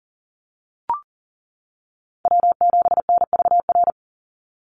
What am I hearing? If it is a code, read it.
Morse code, heard as a keyed tone, about 40 wpm, 690 Hz. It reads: W7NVR
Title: USA Call Signs 40wpm